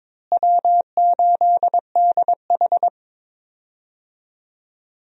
Morse code, heard as a keyed tone, about 22 wpm, 700 Hz. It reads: W8DH
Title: USA Call Signs - 4 Characters - Repeated 2x - 22wpm